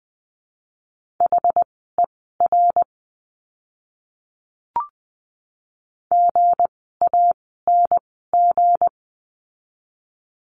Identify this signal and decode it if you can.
Morse code, heard as a keyed tone, about 20 wpm, spaced as for 15 wpm, 700 Hz. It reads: HER GANG
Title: 160 Lesson 26f - Letter g - Words 20wpm